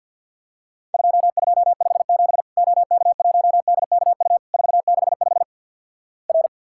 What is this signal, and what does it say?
Morse code, heard as a keyed tone, about 50 wpm, 700 Hz. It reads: 2257 OK1DKU 465 R
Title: Worked All Europe DX Contest - QTC Exchange with Repeats - 50wpm